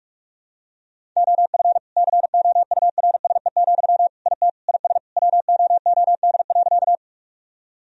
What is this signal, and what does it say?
Morse code, heard as a keyed tone, about 41 wpm, 700 Hz. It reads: OF COURSE, IT IS WOOD.
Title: Sentences from Top 1000 Words - Review Part 2 - 45wpm